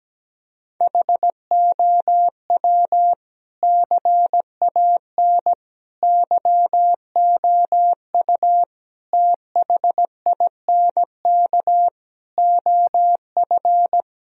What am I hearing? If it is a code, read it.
Morse code, heard as a keyed tone, about 17 wpm, 700 Hz. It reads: HOW CAN YOU THINK OF
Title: Sentences from Top 1000 Words - Review Part 3 - 17wpm